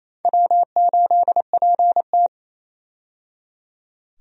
Morse code, heard as a keyed tone, about 27 wpm, 700 Hz. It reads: W8PT
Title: USA Call Signs - 4 Characters - 28wpm